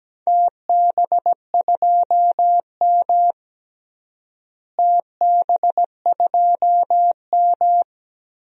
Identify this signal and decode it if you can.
Morse code, heard as a keyed tone, about 17 wpm, 700 Hz. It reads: TB2M TB2M
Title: DXCC-100 Call Signs - 4 Characters - Repeated 2x - 17wpm